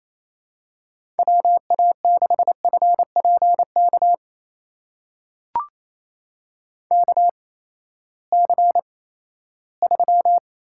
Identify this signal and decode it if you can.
Morse code, heard as a keyed tone, about 27 wpm, spaced as for 16 wpm, 700 Hz. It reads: WA6FPK KC3
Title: USA Call Signs - 6 Characters - 8x Character Spacing - 28wpm